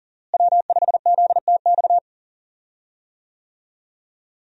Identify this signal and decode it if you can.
Morse code, heard as a keyed tone, about 39 wpm, 700 Hz. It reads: W5ZTX
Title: USA Call Signs - 5 Characters - Repeated 2x - 40wpm